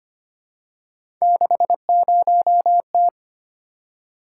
Morse code, heard as a keyed tone, about 25 wpm, 700 Hz. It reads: 60T
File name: Three Letter-Number Combinations - 25wpm